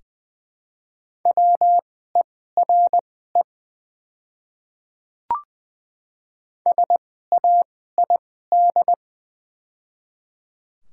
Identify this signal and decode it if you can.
Morse code, heard as a keyed tone, about 20 wpm, spaced as for 14 wpm, 700 Hz. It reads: WERE SAID
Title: 2x Character Spacing - Top 100 Words - 20wpm